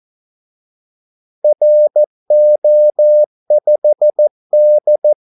Morse code, heard as a keyed tone, about 14 wpm, 600 Hz. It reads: RO5D
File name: Callsigns-14wpm